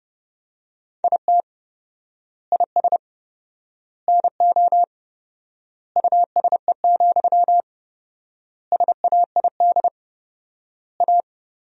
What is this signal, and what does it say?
Morse code, heard as a keyed tone, about 30 wpm, 700 Hz. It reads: IT IS NO USE, SAID A
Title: 4x Word-Spacing - Sentences from Top 500 Words - 30wpm